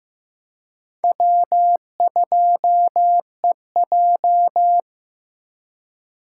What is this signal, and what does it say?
Morse code, heard as a keyed tone, about 18 wpm, 700 Hz. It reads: W2EJ
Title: Speed Racing - USA 1x2 Call Signs 15wpm